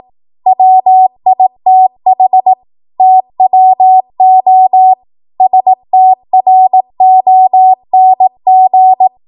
Morse code, heard as a keyed tone, about 18 wpm, 750 Hz. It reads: WITH TWO STRONG
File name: Morse / 18WPM